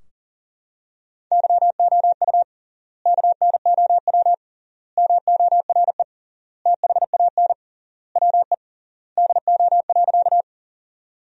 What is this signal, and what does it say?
Morse code, heard as a keyed tone, about 40 wpm, 700 Hz. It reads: YOU KNOW MORE THAN WE DO.
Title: 3x Word-Spacing - Sentences from Top 100 Words - 40wpm